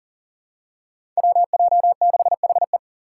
Morse code, heard as a keyed tone, about 37 wpm, 700 Hz. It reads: WJ6HE
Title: Five Challenging Letter-Number Combinations 40wpm